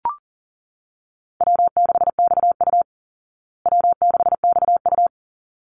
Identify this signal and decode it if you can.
Morse code, heard as a keyed tone, about 40 wpm, 700 Hz. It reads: W6XU W6XU
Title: Call Signs Repeated 3x - 40wpm